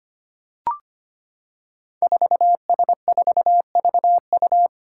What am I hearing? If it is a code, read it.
Morse code, heard as a keyed tone, about 25 wpm, 700 Hz. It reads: 4S4VU
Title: Five Challenging Letter-Number Combinations 25wpm